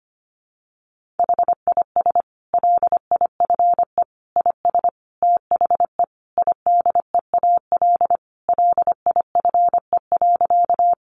Morse code, heard as a keyed tone, about 25 wpm, 700 Hz. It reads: HIS LIFE IS THE IDEAL LIFE.
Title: Sentences from Top 1400 Words 25wpm